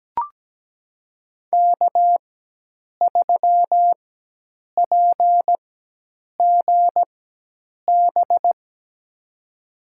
Morse code, heard as a keyed tone, about 17 wpm, spaced as for 8 wpm, 700 Hz. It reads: K3PGB
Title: USA Call Signs - 5 Characters - 4x Character Spacing - 17wpm